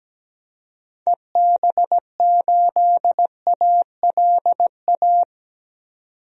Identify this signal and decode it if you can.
Morse code, heard as a keyed tone, about 17 wpm, 700 Hz. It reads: EB8ALA
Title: DXCC-100 Call Signs - 6 Characters - 17wpm